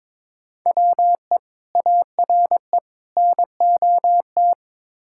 Morse code, heard as a keyed tone, about 21 wpm, 700 Hz. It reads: WE ARE NOT
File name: practice / 22wpm / Build A Phrase - 5-Word Phrases - 22wpm